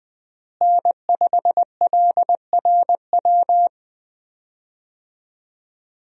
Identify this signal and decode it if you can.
Morse code, heard as a keyed tone, about 20 wpm, 700 Hz. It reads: N5LRW
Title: USA Call Signs - 5 Characters - Repeated 2x - 20wpm